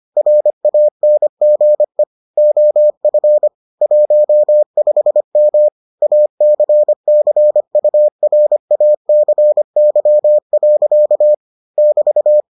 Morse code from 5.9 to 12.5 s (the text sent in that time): ACCURACY. <BT>